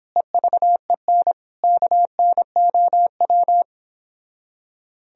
Morse code, heard as a keyed tone, about 29 wpm, 700 Hz. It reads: EVEN KNOW